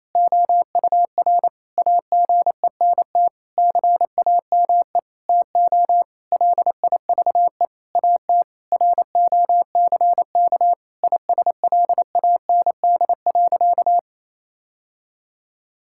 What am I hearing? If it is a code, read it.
Morse code, heard as a keyed tone, about 28 wpm, 700 Hz. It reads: OUR AGENT CAME TO LIVE AT ROCK ISLAND.
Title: Sentences from Top 1600 Words 28wpm